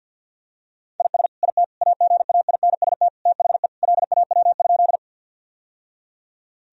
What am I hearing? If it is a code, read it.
Morse code, heard as a keyed tone, about 53 wpm, 700 Hz. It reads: IS IT AGAINST THE LAW?
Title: Sentences from Top 1000 Words - Review Part 3 - 50wpm